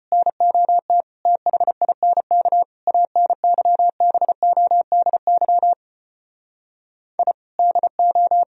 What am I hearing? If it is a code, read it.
Morse code, heard as a keyed tone, about 33 wpm, 700 Hz. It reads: NOT THINK ANYBODY I DO